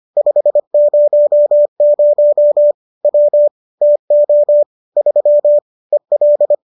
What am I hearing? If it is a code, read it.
Morse code, heard as a keyed tone, about 25 wpm, 590 Hz. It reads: 500 W TO 3 EL
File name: qso213 25